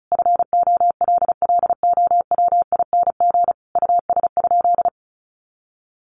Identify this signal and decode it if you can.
Morse code, heard as a keyed tone, about 34 wpm, 700 Hz. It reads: FOLLOWING US?
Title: Sentences from Top 700 Words 35wpm